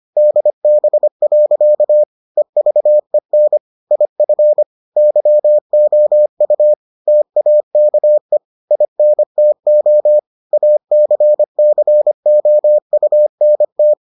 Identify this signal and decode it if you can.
Morse code, heard as a keyed tone, about 25 wpm, 600 Hz. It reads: DB. EVEN IF YOU TAKE INTO ACCOUNT